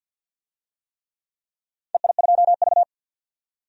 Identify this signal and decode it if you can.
Morse code, heard as a keyed tone, about 47 wpm, 700 Hz. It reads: EI2V
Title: DXCC-100 Call Signs - 4 Characters - 50wpm